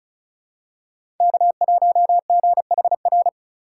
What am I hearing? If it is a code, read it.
Morse code, heard as a keyed tone, about 34 wpm, 700 Hz. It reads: K1GHR
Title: USA Call Signs - 5 Characters - 4x Character Spacing - 35wpm